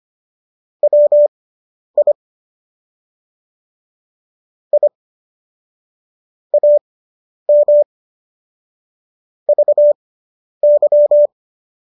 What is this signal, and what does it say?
Morse code, heard as a keyed tone, about 25 wpm, spaced as for 10 wpm, 590 Hz. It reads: WI I AM VY